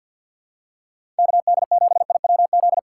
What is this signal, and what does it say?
Morse code, heard as a keyed tone, about 53 wpm, 700 Hz. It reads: KD7IWZ
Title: USA Call Signs - 6 Characters - 6x Character Spacing - 50wpm